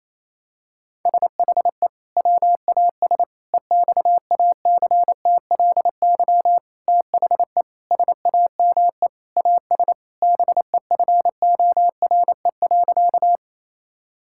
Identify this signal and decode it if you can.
Morse code, heard as a keyed tone, about 28 wpm, 700 Hz. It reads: SHE WAS EXACTLY THE SAME AS BEFORE.
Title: Sentences from Top 1000 Words - Review Part 6 - 28wpm